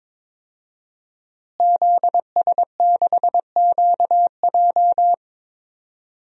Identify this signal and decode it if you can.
Morse code, heard as a keyed tone, about 22 wpm, 700 Hz. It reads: ZS6QJ
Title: DXCC-100 Call Signs - 5 Characters - Repeated 2x - 22wpm